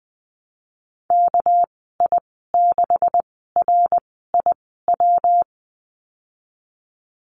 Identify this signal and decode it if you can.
Morse code, heard as a keyed tone, about 20 wpm, spaced as for 15 wpm, 700 Hz. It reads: KI6RIW
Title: 238 Lesson 38g - Letter q - Call Signs 20wpm